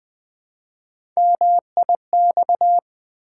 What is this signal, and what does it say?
Morse code, heard as a keyed tone, about 20 wpm, 700 Hz. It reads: MIX
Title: Build A Word - 1 Letter At A Time - 3-Letter Words - 20wpm